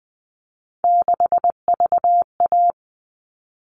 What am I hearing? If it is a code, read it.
Morse code, heard as a keyed tone, about 20 wpm, 700 Hz. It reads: 6VA
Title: All Cut Numbers - 3-Digit 20wpm